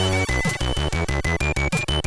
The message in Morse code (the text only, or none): none